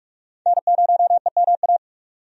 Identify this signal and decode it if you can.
Morse code, heard as a keyed tone, about 46 wpm, 700 Hz. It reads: N0EMA
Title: USA Call Signs - 5 Characters - 45wpm